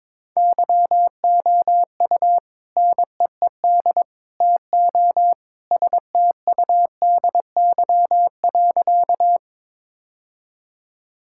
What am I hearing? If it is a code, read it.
Morse code, heard as a keyed tone, about 21 wpm, 700 Hz. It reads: YOU NEED TO STUDY.
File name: practice / 22wpm / Sentences with 3-Word Phrases - 22wpm